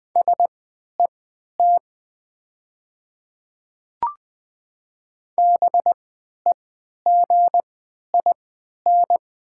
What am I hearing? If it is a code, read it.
Morse code, heard as a keyed tone, about 20 wpm, spaced as for 11 wpm, 690 Hz. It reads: SET BEGIN
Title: 3x Character Spacing - Top 300 Words - 20wpm